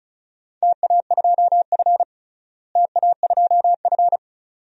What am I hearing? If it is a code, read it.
Morse code, heard as a keyed tone, about 36 wpm, 700 Hz. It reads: TA2F TA2F
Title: DXCC-100 Call Signs - 4 Characters - Repeated 2x - 35wpm